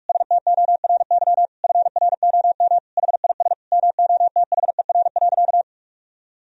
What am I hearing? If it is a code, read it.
Morse code, heard as a keyed tone, about 43 wpm, 700 Hz. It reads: STORY FROM HIS MOTHER.
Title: Sentences from Top 1000 Words - Review Part 4 - 45wpm